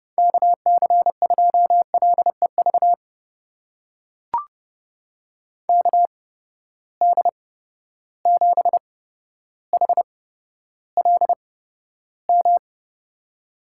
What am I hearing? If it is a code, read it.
Morse code, heard as a keyed tone, about 30 wpm, spaced as for 13 wpm, 700 Hz. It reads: KC2LEV KD7HLM
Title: USA Call Signs - 6 Characters - 8x Character Spacing - 30wpm